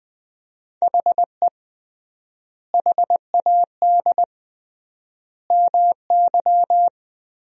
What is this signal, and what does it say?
Morse code, heard as a keyed tone, about 20 wpm, 700 Hz. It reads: HE HAD MY
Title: 3x Word-Spacing - Sentences from Top 400 Words - 20wpm